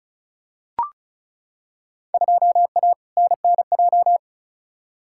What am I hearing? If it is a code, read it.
Morse code, heard as a keyed tone, about 36 wpm, 700 Hz. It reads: 2A NNJ
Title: Field Day Exchanges - 35wpm